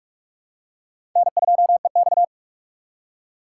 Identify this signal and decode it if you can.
Morse code, heard as a keyed tone, about 43 wpm, 690 Hz. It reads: N2EX
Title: USA Call Signs - 4x Character Spacing - 45wpm